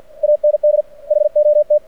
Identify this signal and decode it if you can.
Morse code, heard as a keyed tone, about 49 wpm, 600 Hz. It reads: AND HOT
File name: archived / CRSnet